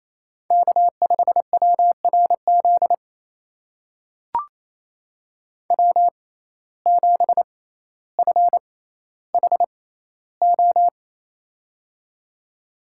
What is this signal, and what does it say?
Morse code, heard as a keyed tone, about 28 wpm, spaced as for 14 wpm, 700 Hz. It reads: K5WRZ W7FHO
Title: USA Call Signs - 5 Characters - 6x Character Spacing - 28wpm